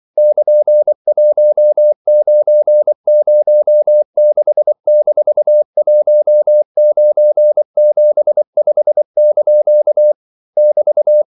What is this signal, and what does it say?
Morse code, heard as a keyed tone, about 24 wpm, 600 Hz. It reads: <KN>1906-1975) <BT>
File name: CW / Practice / EnglishQuotations0224